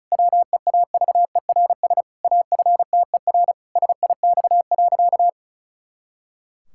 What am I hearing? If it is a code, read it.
Morse code, heard as a keyed tone, about 34 wpm, 700 Hz. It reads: WEAVERS AFTER SIX.
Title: Pangram Challenge v2 - 35wpm